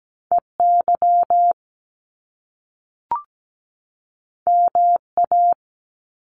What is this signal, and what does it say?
Morse code, heard as a keyed tone, about 17 wpm, 700 Hz. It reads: EY MA
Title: Top Two Letter Combinations in Words 17wpm